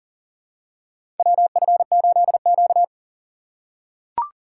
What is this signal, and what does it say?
Morse code, heard as a keyed tone, about 40 wpm, 700 Hz. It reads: WF8Q
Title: USA Call Signs - 4 Characters - 3x Character Spacing - 40wpm